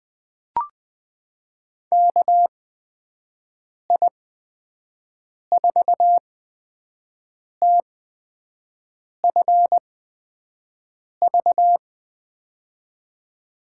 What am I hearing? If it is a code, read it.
Morse code, heard as a keyed tone, about 20 wpm, spaced as for 5 wpm, 700 Hz. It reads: KI4TFV
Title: USA Call Signs - 6 Characters - 8x Character Spacing - 20wpm